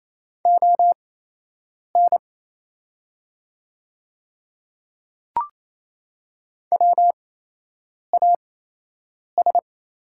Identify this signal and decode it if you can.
Morse code, heard as a keyed tone, about 28 wpm, spaced as for 8 wpm, 700 Hz. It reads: ON WAS
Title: 8x Character Spacing - Top 100 Words - 28wpm